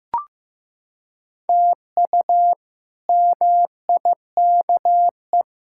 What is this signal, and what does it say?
Morse code, heard as a keyed tone, about 15 wpm, 700 Hz. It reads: TU MIKE
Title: K1USN SST QSO Practice - 15wpm